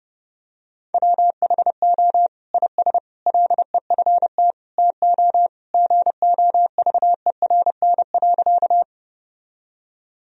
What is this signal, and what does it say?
Morse code, heard as a keyed tone, about 30 wpm, 700 Hz. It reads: WHO IS LEFT TO GOVERN.
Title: Sentences from Top 1000 Words - Review Part 2 - 30wpm